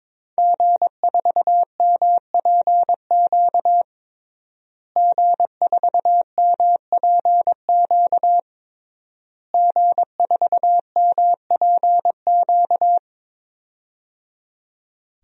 Morse code, heard as a keyed tone, about 22 wpm, 700 Hz. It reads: G4MPQ G4MPQ G4MPQ
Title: DXCC-100 Call Signs - 5 Characters - Repeated 3x - 22wpm